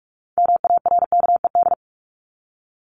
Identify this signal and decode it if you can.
Morse code, heard as a keyed tone, about 44 wpm, 700 Hz. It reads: MARKED
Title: ED Suffix - Top 100 Words with Base Words 45wpm